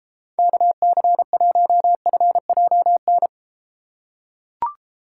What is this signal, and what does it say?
Morse code, heard as a keyed tone, about 32 wpm, 700 Hz. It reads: KC1FJN